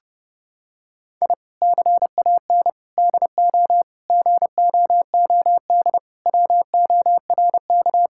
Morse code, heard as a keyed tone, about 30 wpm, 700 Hz. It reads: I CAN DO GOOD WORK
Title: Sentences from Top 1000 Words - Review Part 6 - 30wpm